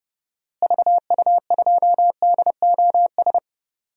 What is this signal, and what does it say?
Morse code, heard as a keyed tone, about 32 wpm, 700 Hz. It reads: VU2DOS
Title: DXCC-100 Call Signs - 6 Characters - Repeated 2x - 30wpm